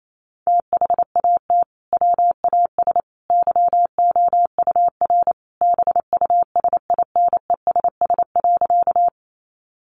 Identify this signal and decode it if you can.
Morse code, heard as a keyed tone, about 28 wpm, 700 Hz. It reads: THAT WAS YOUR BUSINESS.
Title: Sentences from Top 600 Words 28wpm